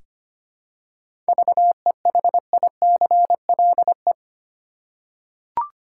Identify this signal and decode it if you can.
Morse code, heard as a keyed tone, about 25 wpm, 700 Hz. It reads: VEHICLE
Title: Seven Letter Words - 25wpm